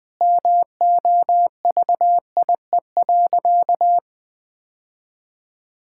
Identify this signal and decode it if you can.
Morse code, heard as a keyed tone, about 20 wpm, 700 Hz. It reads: MOVIE.
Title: Sentences with 2-Word Phrases - 20wpm